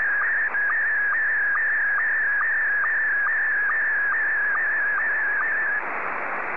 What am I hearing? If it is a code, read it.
SSTV (slow-scan television), Scottie 1.